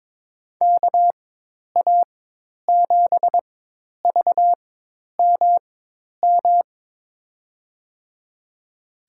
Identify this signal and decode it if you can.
Morse code, heard as a keyed tone, about 22 wpm, spaced as for 10 wpm, 700 Hz. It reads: KA7VMM
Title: USA Call Signs - 6 Characters - 4x Character Spacing - 22wpm